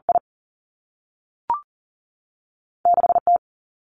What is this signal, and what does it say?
Morse code, heard as a keyed tone, about 38 wpm, 700 Hz. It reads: I 6 T
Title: Two Challenging Letter-Number Combinations 40wpm